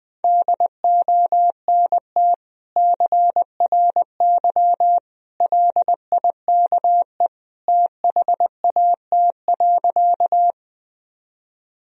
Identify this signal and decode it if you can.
Morse code, heard as a keyed tone, about 20 wpm, 700 Hz. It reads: DONT CRY LIKE THAT.
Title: Sentences from Top 1000 Words - Review Part 2 - 20wpm